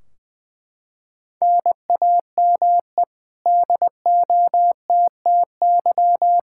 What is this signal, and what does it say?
Morse code, heard as a keyed tone, about 21 wpm, 700 Hz. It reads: NAME DOTTY
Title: K1USN SST QSO Practice 20wpm